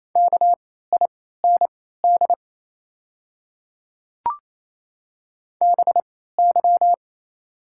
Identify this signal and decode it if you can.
Morse code, heard as a keyed tone, about 28 wpm, spaced as for 16 wpm, 700 Hz. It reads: KIND BY
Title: 3x Character Spacing - Top 300 Words - 28wpm